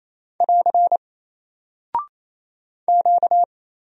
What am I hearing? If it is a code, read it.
Morse code, heard as a keyed tone, about 27 wpm, 700 Hz. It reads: <AR> Q
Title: Single Character (Letter:Number:Punctuation:Prosign) Alt 28wpm